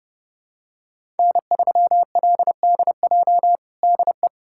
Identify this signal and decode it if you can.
Morse code, heard as a keyed tone, about 30 wpm, 700 Hz. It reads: N3LDJ DE
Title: USA Callsign DE Callsign - 5 Characters - 30wpm